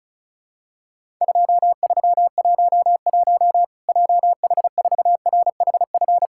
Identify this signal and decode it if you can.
Morse code, heard as a keyed tone, about 36 wpm, 700 Hz. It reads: 2311 JH4RHF
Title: Worked All Europe DX Contest - QTC Exchange - 35wpm